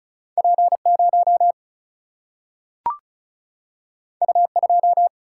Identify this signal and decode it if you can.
Morse code, heard as a keyed tone, about 36 wpm, 700 Hz. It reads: P 0 U 2
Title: Single-Letter Single-Number 35wpm